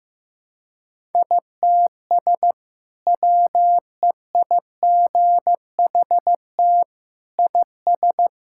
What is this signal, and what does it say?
Morse code, heard as a keyed tone, about 15 wpm, 700 Hz. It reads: ITS WEIGHT IS
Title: Sentences from Top 1000 Words - Review Part 6 - 15wpm